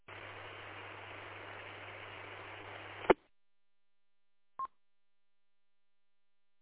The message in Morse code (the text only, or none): none